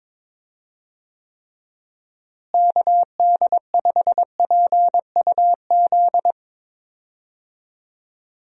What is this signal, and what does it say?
Morse code, heard as a keyed tone, about 22 wpm, 700 Hz. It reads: KD5PUZ
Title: USA Call Signs - 6 Characters - Repeated 2x - 22wpm